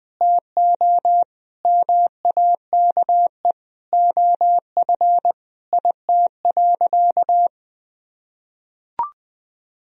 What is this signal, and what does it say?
Morse code, heard as a keyed tone, about 20 wpm, 700 Hz. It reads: TO MAKE OF IT.
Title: Sentences from Top 1000 Words - Review Part 1 - 20wpm